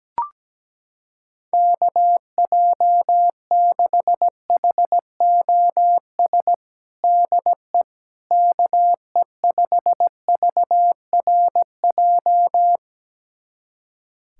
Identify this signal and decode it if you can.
Morse code, heard as a keyed tone, about 17 wpm, 700 Hz. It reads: KJ6HOS DE KE5VRJ
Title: USA Callsign DE Callsign - 6 Characters - 17wpm